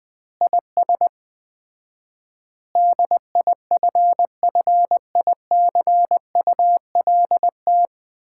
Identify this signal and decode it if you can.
Morse code, heard as a keyed tone, about 20 wpm, 700 Hz. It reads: IS DIFFICULT
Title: 4x Word-Spacing - Sets of 4 Words - 20wpm